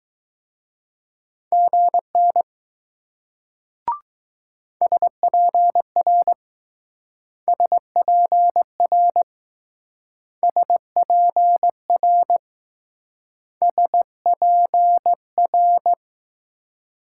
Morse code, heard as a keyed tone, about 20 wpm, 700 Hz. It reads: GN SPR SPR SPR SPR